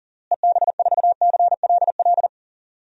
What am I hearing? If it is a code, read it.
Morse code, heard as a keyed tone, about 37 wpm, 700 Hz. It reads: EB4CLL
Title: DXCC-100 Call Signs - 6 Characters - Repeated 3x - 40wpm